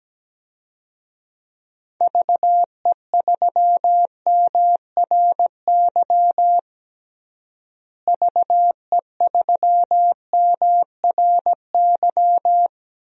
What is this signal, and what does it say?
Morse code, heard as a keyed tone, about 17 wpm, 700 Hz. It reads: VE3MRY VE3MRY
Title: DXCC-100 Call Signs - 6 Characters - Repeated 3x - 17wpm